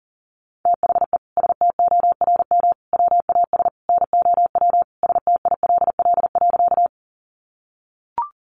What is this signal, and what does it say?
Morse code, heard as a keyed tone, about 40 wpm, 700 Hz. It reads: THE STORM WAS NOW STILL.
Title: Sentences from Top 1500 Words 40wpm